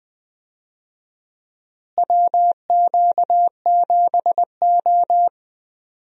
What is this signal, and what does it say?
Morse code, heard as a keyed tone, about 20 wpm, 700 Hz. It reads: WQ7O
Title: USA Call Signs - 4 Characters - Repeated 2x - 20wpm